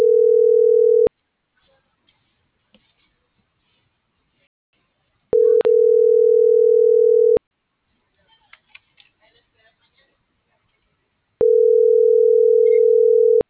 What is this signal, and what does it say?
Ringback tone (North America), 440 + 480 Hz.